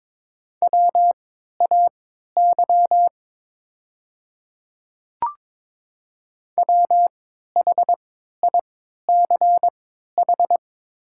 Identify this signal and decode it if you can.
Morse code, heard as a keyed tone, about 22 wpm, spaced as for 12 wpm, 700 Hz. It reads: WAY WHICH
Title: 3x Character Spacing - Top 100 Words - 22wpm